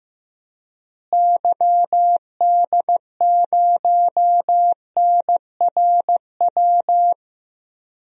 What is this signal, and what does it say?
Morse code, heard as a keyed tone, about 16 wpm, 700 Hz. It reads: YD0NRW
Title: DXCC-100 Call Signs - 6 Characters - Repeated 3x - 15wpm